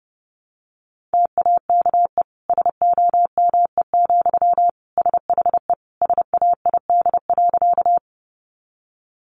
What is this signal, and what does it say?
Morse code, heard as a keyed tone, about 30 wpm, 700 Hz. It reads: TAKE SOME, SHE SAID.
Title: Sentences from Top 200 Words 30wpm